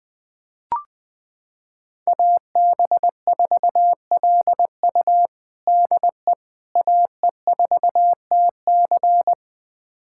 Morse code, heard as a keyed tone, about 20 wpm, 700 Hz. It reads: AB4LU DE AE4TC
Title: USA Callsign DE Callsign - 5 Characters - 20wpm